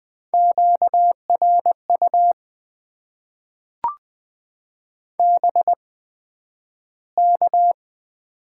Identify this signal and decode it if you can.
Morse code, heard as a keyed tone, about 20 wpm, spaced as for 11 wpm, 700 Hz. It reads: QRU BK
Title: Top 90 QSO Elements - 8x Character Spacing - 20wpm